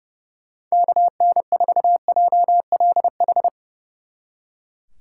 Morse code, heard as a keyed tone, about 36 wpm, 700 Hz. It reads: KN4JLH
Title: Speed Racing - USA 2x3 Call Signs 30wpm